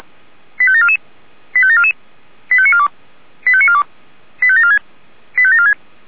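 Five-tone selective call (ZVEI), heard as quick runs of five tones, tones of about 70 ms. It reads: 85744 85744 85942 85942 85746 85746
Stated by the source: Received by SDR, ZVEI five-tone selective call 85725.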